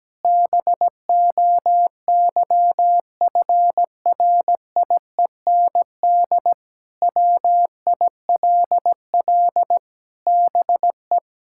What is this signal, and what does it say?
Morse code, heard as a keyed tone, about 17 wpm, 700 Hz. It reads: BOYFRIEND WILL BE
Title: Sentences from Top 1000 Words - Review Part 5 - 17wpm